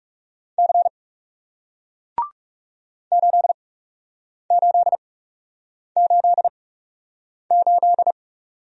Morse code, heard as a keyed tone, about 41 wpm, spaced as for 8 wpm, 700 Hz. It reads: C 8888